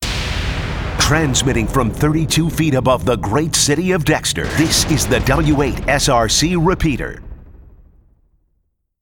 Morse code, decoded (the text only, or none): none